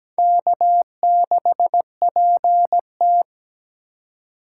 Morse code, heard as a keyed tone, about 17 wpm, 700 Hz. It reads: K6PT
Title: USA Call Signs - 2x Character Spacing - 17wpm